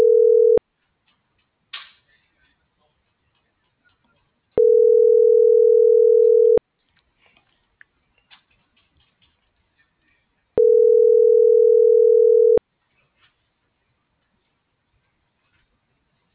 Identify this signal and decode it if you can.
Ringback tone (North America), 440 + 480 Hz.